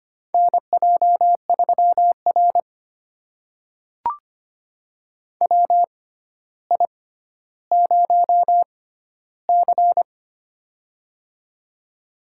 Morse code, heard as a keyed tone, about 25 wpm, spaced as for 13 wpm, 700 Hz. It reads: NJ3R WI0C
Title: USA Call Signs - 4 Characters - 6x Character Spacing - 25wpm